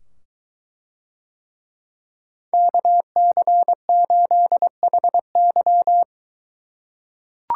Morse code, heard as a keyed tone, about 22 wpm, 700 Hz. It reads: KC8HY